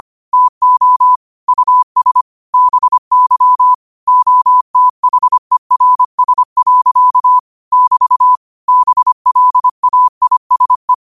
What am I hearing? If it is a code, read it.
Morse code, heard as a keyed tone, about 25 wpm, 1000 Hz. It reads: TO US BY OTHERS. <BT> BLAISE